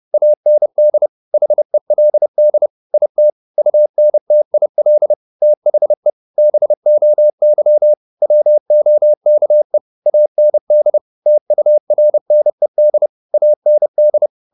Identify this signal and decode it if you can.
Morse code, heard as a keyed tone, about 30 wpm, 600 Hz. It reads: AND HELD IT UNTIL THE BOY WOKE AND TURNED AND
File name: the old man and the sea / the old man and the sea 1 30wpm